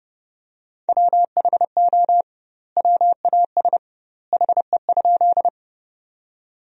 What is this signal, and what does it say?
Morse code, heard as a keyed tone, about 30 wpm, 700 Hz. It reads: WHO WAS HE?
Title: 2x Word-Spacing - Sentences from Top 100 Words - 30wpm